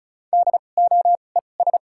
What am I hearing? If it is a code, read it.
Morse code, heard as a keyed tone, about 36 wpm, spaced as for 25 wpm, 700 Hz. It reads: DOES
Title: 2x Character Spacing - Top 200 Words - 35wpm